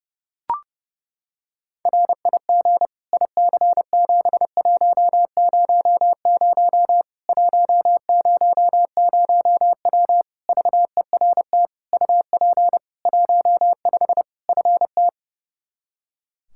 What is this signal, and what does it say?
Morse code, heard as a keyed tone, about 32 wpm, 700 Hz. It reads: RIG IC7100 100W VERT UP 15 FT
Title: K1USN SST QSO Practice 30wpm